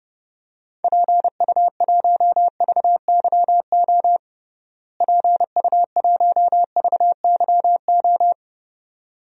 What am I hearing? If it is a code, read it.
Morse code, heard as a keyed tone, about 32 wpm, 700 Hz. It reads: PU1VYO PU1VYO
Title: DXCC-100 Call Signs - 6 Characters - Repeated 2x - 30wpm